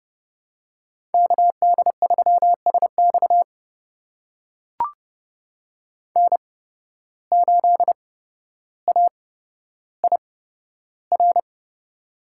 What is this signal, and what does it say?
Morse code, heard as a keyed tone, about 31 wpm, spaced as for 13 wpm, 700 Hz. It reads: KD3SX N8AIR
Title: USA Call Signs - 5 Characters - 8x Character Spacing - 30wpm